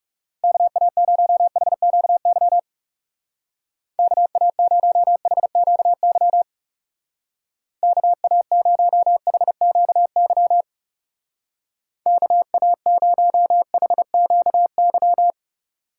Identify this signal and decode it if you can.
Morse code, heard as a keyed tone, about 37 wpm, 700 Hz. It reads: KA0HQY KA0HQY KA0HQY KA0HQY
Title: Speed Racing - USA 2x3 Call Signs 30wpm